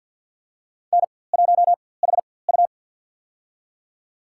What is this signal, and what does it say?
Morse code, heard as a keyed tone, about 46 wpm, spaced as for 23 wpm, 700 Hz. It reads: N1HU
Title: USA Call Signs - 4x Character Spacing - 50wpm